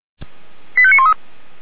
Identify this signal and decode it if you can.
Five-tone selective call (ZVEI), heard as quick runs of five tones, tones of about 70 ms. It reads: 85913